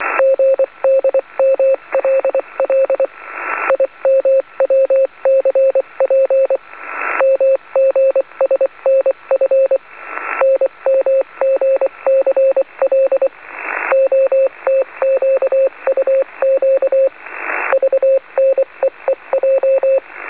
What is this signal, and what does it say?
Morse code, heard as a keyed tone, about 24 wpm, 540 Hz, with radio noise. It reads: GDMLL IMWCP MGSNF NMGCL OTQUQ VNEEJ
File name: 4.440-CW-in-USB-niet-mee-te-lezen-15-06-2012-1